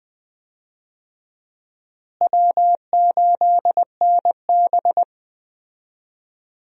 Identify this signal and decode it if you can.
Morse code, heard as a keyed tone, about 20 wpm, 700 Hz. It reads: W8NB
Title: USA Call Signs - 4 Characters - Repeated 2x - 20wpm